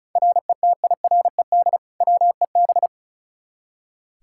Morse code, heard as a keyed tone, about 34 wpm, 700 Hz. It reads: RETIRED WEB
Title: K1USN SST QSO Practice 35wpm